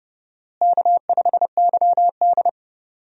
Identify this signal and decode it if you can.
Morse code, heard as a keyed tone, about 30 wpm, 700 Hz. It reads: K5YD
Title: USA Call Signs - 4 Characters - Repeated 3x - 30wpm